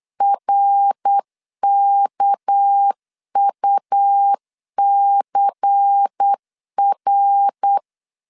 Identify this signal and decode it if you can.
Morse code, heard as a keyed tone, about 8 wpm, 800 Hz. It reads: RKUCR
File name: CW / DET / 1 6 1500C S A 8wpm 5203